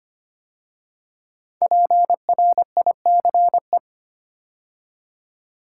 Morse code, heard as a keyed tone, about 25 wpm, 700 Hz. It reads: PRICE
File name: 5 Letter Words - First 3 Letters Given - 25wpm